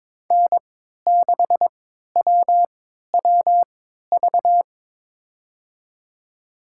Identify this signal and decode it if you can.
Morse code, heard as a keyed tone, about 23 wpm, spaced as for 13 wpm, 700 Hz. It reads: N6WWV
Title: USA Call Signs - 5 Characters - 3x Character Spacing - 22wpm